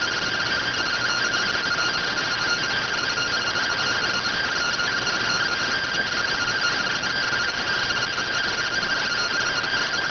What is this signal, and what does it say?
Two-tone teleprinter signal, about 170 Hz shift.